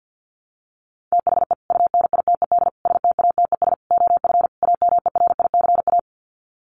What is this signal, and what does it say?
Morse code, heard as a keyed tone, about 50 wpm, 700 Hz. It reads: THE UNITED STATES OF AMERICA
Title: Sets of 5 Words 50wpm